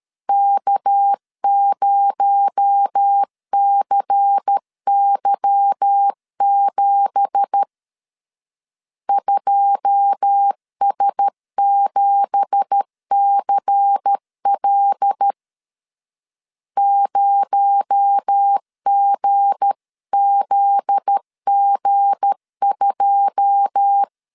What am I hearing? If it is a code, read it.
Morse code, heard as a keyed tone, about 13 wpm, 800 Hz. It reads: K0CY7 2S7CL 0GZG2
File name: LEZIONI / 9 2867 12wpm 2322